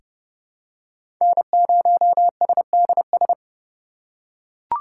Morse code, heard as a keyed tone, about 32 wpm, 700 Hz. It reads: N0SDS
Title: USA Call Signs - 5 Characters - 4x Character Spacing - 30wpm